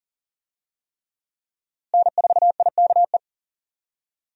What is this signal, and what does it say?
Morse code, heard as a keyed tone, about 37 wpm, 700 Hz. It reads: N4IKE
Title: USA Call Signs - 5 Characters - Repeated 3x - 40wpm